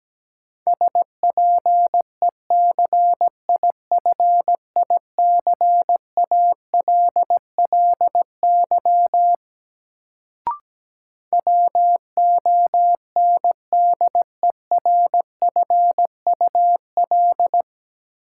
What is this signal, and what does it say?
Morse code, heard as a keyed tone, about 17 wpm, 700 Hz. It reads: SPECIFICALLY WONDERFUL
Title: Sentences from Top 1100 Words - New Word Drill Alt - 17wpm